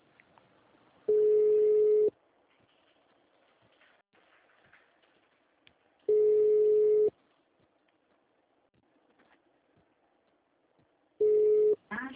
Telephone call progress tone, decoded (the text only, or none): ringback tone